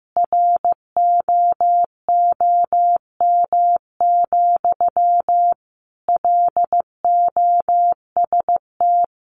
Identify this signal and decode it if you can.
Morse code, heard as a keyed tone, about 14 wpm, 700 Hz. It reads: ROOM, LOST
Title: Sentences from Top 1400 Words 15wpm